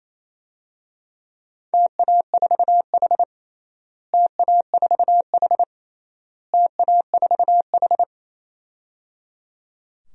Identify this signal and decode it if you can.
Morse code, heard as a keyed tone, about 28 wpm, 700 Hz. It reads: TA4H TA4H TA4H
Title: DXCC-100 Call Signs - 4 Characters - Repeated 3x - 28wpm